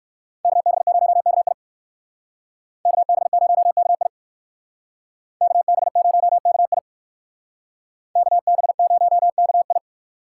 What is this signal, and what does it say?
Morse code, heard as a keyed tone, about 61 wpm, 700 Hz. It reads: KB0KI KB0KI KB0KI KB0KI